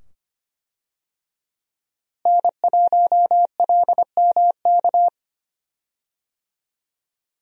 Morse code, heard as a keyed tone, about 25 wpm, 700 Hz. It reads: N1LMK
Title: USA Call Signs - 5 Characters - Repeated 2x - 25wpm